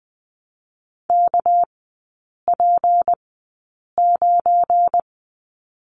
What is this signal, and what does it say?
Morse code, heard as a keyed tone, about 20 wpm, spaced as for 8 wpm, 700 Hz. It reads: KP9
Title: 205 Lesson 33e - Letter j - Three Characters 20wpm